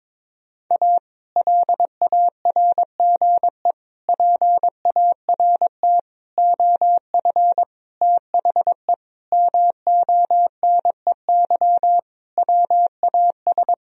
Morse code, heard as a keyed tone, about 22 wpm, 700 Hz. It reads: A LARGE PART OF THE MONEY WAS
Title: Sentences from Top 1000 Words - Review Part 1 - 22wpm